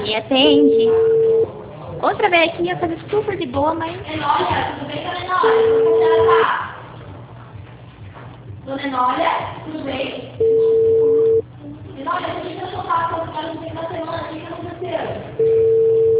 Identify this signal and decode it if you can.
Ringback tone (China, 450 Hz), with speech.